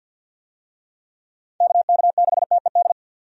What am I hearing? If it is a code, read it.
Morse code, heard as a keyed tone, about 53 wpm, 700 Hz. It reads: KK6TED
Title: USA Call Signs - 6 Characters - 50wpm